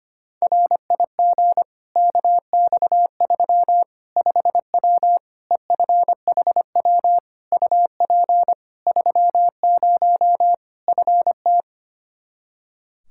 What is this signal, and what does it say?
Morse code, heard as a keyed tone, about 25 wpm, 700 Hz. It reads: RIG KX3 5W EFHW UP 30 FT
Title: Practice QSO - Exchange 2 - Info - 25wpm